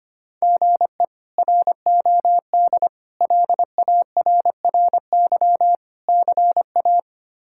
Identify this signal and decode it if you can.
Morse code, heard as a keyed tone, about 25 wpm, 700 Hz. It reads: GE ROD LARRY CA
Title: K1USN SST QSO Practice - 25wpm